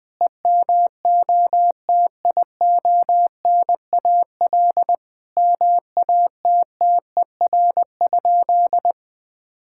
Morse code, heard as a keyed tone, about 20 wpm, 700 Hz. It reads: EMOTIONAL MATTER?
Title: Sentences from Top 1000 Words - Review Part 5 - 20wpm